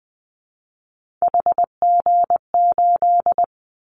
Morse code, heard as a keyed tone, about 20 wpm, 700 Hz. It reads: HG8
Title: Callsign Prefixes for Top 25 DXCC - Repeated 3x - 20wpm